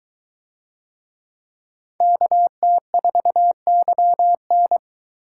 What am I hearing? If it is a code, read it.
Morse code, heard as a keyed tone, about 21 wpm, 700 Hz. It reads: KT4YN